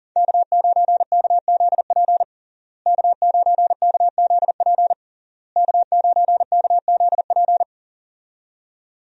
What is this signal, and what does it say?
Morse code, heard as a keyed tone, about 37 wpm, 700 Hz. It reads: K9KZP K9KZP K9KZP
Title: USA Call Signs - 5 Characters - Repeated 3x - 40wpm